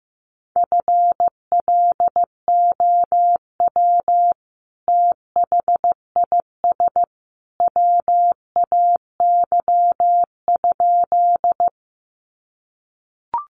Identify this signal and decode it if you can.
Morse code, heard as a keyed tone, about 15 wpm, 700 Hz. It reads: FLOW THIS WAY?
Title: Sentences from Top 1200 Words 15wpm